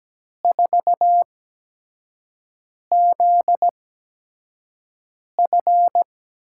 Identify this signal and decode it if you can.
Morse code, heard as a keyed tone, about 17 wpm, spaced as for 5 wpm, 700 Hz. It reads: 4ZF
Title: USA Call Signs - 4 Characters - 8x Character Spacing - 17wpm